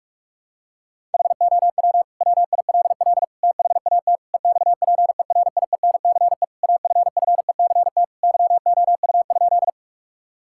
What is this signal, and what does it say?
Morse code, heard as a keyed tone, about 46 wpm, 700 Hz. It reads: HOW WILL THAT EXPERIENCE AFFECT YOU?
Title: Sentences from Top 1000 Words - Review Part 6 - 45wpm